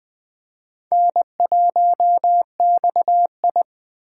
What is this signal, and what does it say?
Morse code, heard as a keyed tone, about 19 wpm, 700 Hz. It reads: N1XI